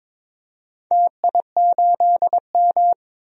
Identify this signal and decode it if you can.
Morse code, heard as a keyed tone, about 22 wpm, 700 Hz. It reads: TI8M
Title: DXCC-100 Call Signs - 4 Characters - Repeated 2x - 22wpm